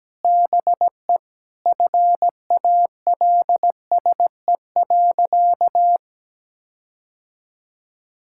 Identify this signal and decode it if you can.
Morse code, heard as a keyed tone, about 17 wpm, 700 Hz. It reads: BE FALSE.
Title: Sentences from Top 1600 Words 17wpm